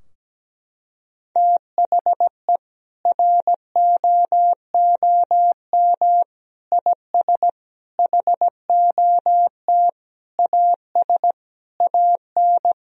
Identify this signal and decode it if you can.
Morse code, heard as a keyed tone, about 17 wpm, 700 Hz. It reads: THE ROOM IS HOT AS AN
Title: Sentences from Top 1000 Words - Review Part 4 - 17wpm